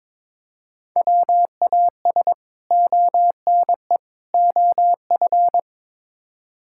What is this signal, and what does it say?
Morse code, heard as a keyed tone, about 22 wpm, 700 Hz. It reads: WAS ONE OF
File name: practice / 22wpm / Build A Phrase - 5-Word Phrases - 22wpm